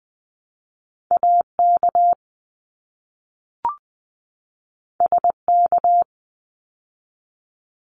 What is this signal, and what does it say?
Morse code, heard as a keyed tone, about 20 wpm, 700 Hz. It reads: AK SK
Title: ARRL Sections 20wpm